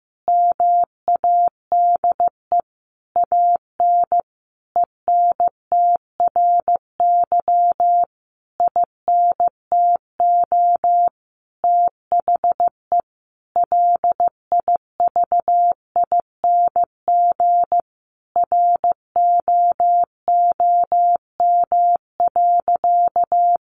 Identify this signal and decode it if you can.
Morse code, heard as a keyed tone, about 15 wpm, 700 Hz. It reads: MADE AN ENTRY INTO THE LIVING ROOM.
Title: Sentences from Top 1500 Words 15wpm